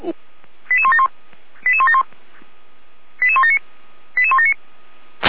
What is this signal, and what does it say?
Five-tone selective call (ZVEI), heard as quick runs of five tones, tones of about 70 ms, with speech. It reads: 88171 88171 88179 88179